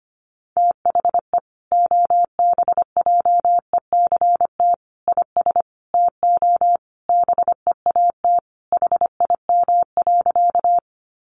Morse code, heard as a keyed tone, about 25 wpm, 700 Hz. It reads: THE OBJECT IS TO BEAT HIM.
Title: Sentences from Top 1300 Words 25wpm